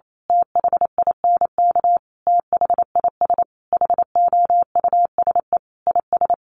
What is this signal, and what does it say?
Morse code, heard as a keyed tone, about 28 wpm, 700 Hz. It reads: THINK THIS HOUSE IS
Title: Sentences from Top 1000 Words 28wpm